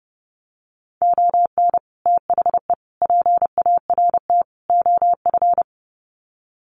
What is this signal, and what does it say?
Morse code, heard as a keyed tone, about 30 wpm, 700 Hz. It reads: ON THE PART OF
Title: Sets of 4 Words 30wpm